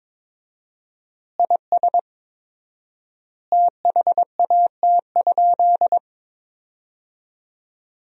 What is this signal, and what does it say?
Morse code, heard as a keyed tone, about 22 wpm, 700 Hz. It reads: IS THAT?
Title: 4x Word-Spacing - Sentences from Top 100 Words - 22wpm